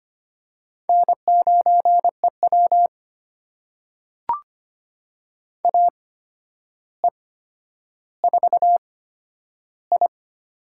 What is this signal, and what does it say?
Morse code, heard as a keyed tone, about 25 wpm, spaced as for 11 wpm, 700 Hz. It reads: N9EW AE4I
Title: USA Call Signs - 8x Character Spacing - 25wpm